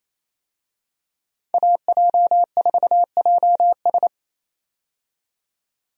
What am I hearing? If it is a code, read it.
Morse code, heard as a keyed tone, about 28 wpm, 700 Hz. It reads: AJ4JS
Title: USA Call Signs - 5 Characters - 28wpm